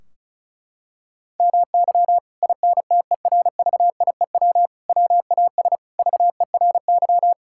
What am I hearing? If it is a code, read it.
Morse code, heard as a keyed tone, about 36 wpm, 700 Hz. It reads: MY INTERVIEW WAS VERY
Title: Sentences from Top 1600 Words 35wpm